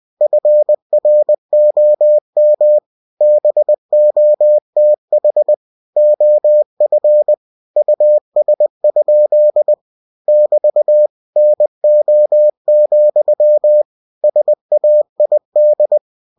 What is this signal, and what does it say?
Morse code, heard as a keyed tone, about 20 wpm, 600 Hz. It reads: FROM BOTH OF US? <BT> NO, SAID
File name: cw-books / winnie-the-pooh / winnie-20wpm-page-0049